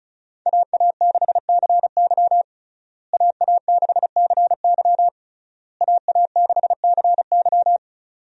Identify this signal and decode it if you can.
Morse code, heard as a keyed tone, about 36 wpm, 700 Hz. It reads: AA6CY AA6CY AA6CY
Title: USA Call Signs - 5 Characters - Repeated 3x - 35wpm